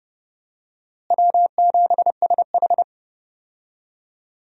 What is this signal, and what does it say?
Morse code, heard as a keyed tone, about 32 wpm, 700 Hz. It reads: W7SH
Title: USA Call Signs - 4 Characters - Repeated 3x - 30wpm